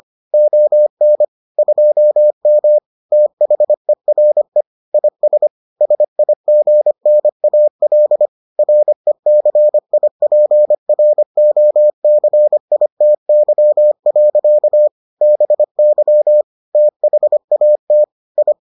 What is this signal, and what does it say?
Morse code, heard as a keyed tone, about 25 wpm, 600 Hz. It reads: ON 2M THERE IS SIGNAL RECIPROCITY. BY THAT I